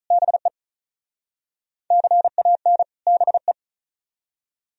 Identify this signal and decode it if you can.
Morse code, heard as a keyed tone, about 44 wpm, 700 Hz. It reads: BE CAN BE